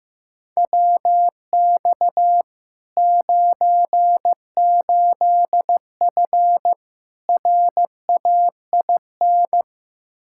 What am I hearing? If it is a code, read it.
Morse code, heard as a keyed tone, about 15 wpm, 700 Hz. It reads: WX 98F RAIN
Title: Practice QSO - Exchange 2 - Info - 15wpm